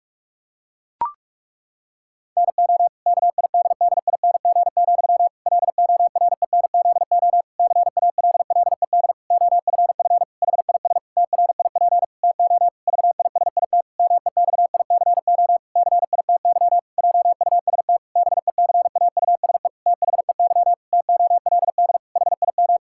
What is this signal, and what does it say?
Morse code, heard as a keyed tone, about 48 wpm, 700 Hz. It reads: NO KIDDING, LORENZO CALLED OFF HIS TRIP TO VISIT MEXICO CITY JUST BECAUSE THEY TOLD HIM
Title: Pangram Challenge v2 - 45wpm